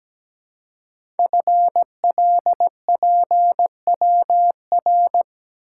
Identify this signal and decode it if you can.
Morse code, heard as a keyed tone, about 17 wpm, 700 Hz. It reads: FLPWR
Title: Code Groups - 5 Letters - 17wpm